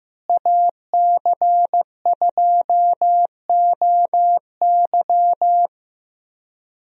Morse code, heard as a keyed tone, about 15 wpm, 700 Hz. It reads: AC2OY
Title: USA Call Signs - 5 Characters - Repeated 2x - 15wpm